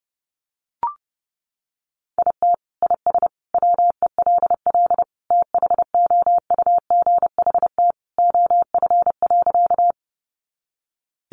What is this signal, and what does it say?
Morse code, heard as a keyed tone, about 30 wpm, 700 Hz. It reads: IT IS WELL THOUGHT OF.
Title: Sentences from Top 300 Words 30wpm